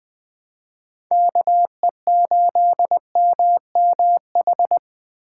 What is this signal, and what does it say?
Morse code, heard as a keyed tone, about 20 wpm, 700 Hz. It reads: KE8MMH
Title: USA Call Signs - 6 Characters - 2x Character Spacing - 20wpm